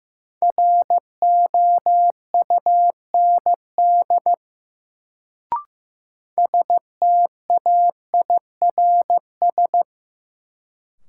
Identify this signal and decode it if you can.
Morse code, heard as a keyed tone, about 15 wpm, 700 Hz. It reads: ROUND STAIRS
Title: Sets of 5 Words Encore - New Word Drill - 15wpm